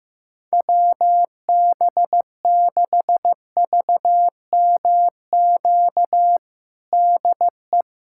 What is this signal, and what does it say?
Morse code, heard as a keyed tone, about 15 wpm, 700 Hz. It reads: WB6VMQ DE
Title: USA Callsign DE Callsign - 6 Characters - 15wpm